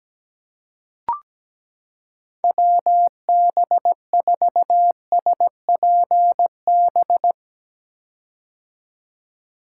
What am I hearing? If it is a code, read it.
Morse code, heard as a keyed tone, about 17 wpm, 700 Hz. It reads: WB4SPB
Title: K1USN SST QSO Practice - 17wpm